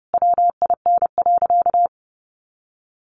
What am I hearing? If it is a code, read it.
Morse code, heard as a keyed tone, about 30 wpm, 700 Hz. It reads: WIN.
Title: Sentences from Top 1500 Words 30wpm